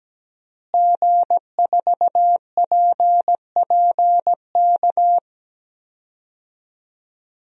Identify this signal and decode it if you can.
Morse code, heard as a keyed tone, about 17 wpm, 700 Hz. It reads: G4PPK
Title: DXCC-100 Call Signs - 5 Characters - Repeated 2x - 17wpm